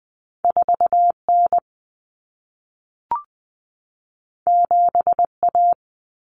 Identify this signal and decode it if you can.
Morse code, heard as a keyed tone, about 20 wpm, 700 Hz. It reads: 4N 7A
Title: Common Cut Numbers - 2-Digit 20wpm